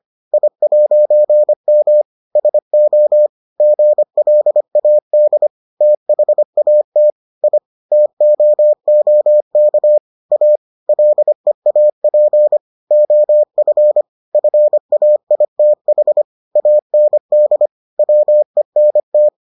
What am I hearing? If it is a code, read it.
Morse code, heard as a keyed tone, about 25 wpm, 600 Hz. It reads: I'M SO GLAD THAT I TOOK A LEAP OF FAITH AND WENT